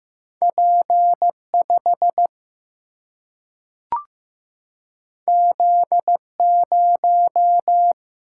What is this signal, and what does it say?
Morse code, heard as a keyed tone, about 15 wpm, 700 Hz. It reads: P 5 Z 0
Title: Single-Letter Single-Number 15wpm